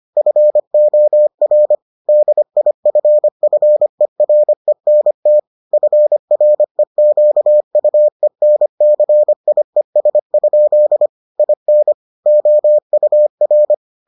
Morse code, heard as a keyed tone, about 25 wpm, 600 Hz. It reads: FOR DIFFERENT FREQUENCIES? IN OUR